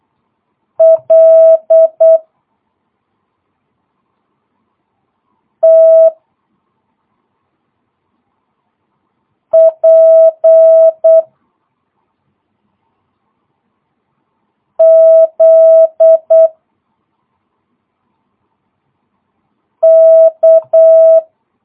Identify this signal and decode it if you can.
Morse code, heard as a keyed tone, about 8 wpm, 650 Hz. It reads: L T P Z K
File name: jh8cbh / 2022cw